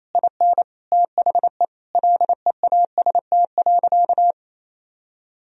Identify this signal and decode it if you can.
Morse code, heard as a keyed tone, about 27 wpm, 700 Hz. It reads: IN THE LEAST.
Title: Sentences from Top 1000 Words - Review Part 4 - 28wpm